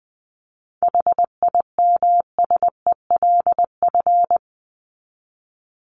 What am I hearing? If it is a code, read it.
Morse code, heard as a keyed tone, about 20 wpm, 700 Hz. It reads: HIMSELF
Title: Top 1501-1750 Words 20wpm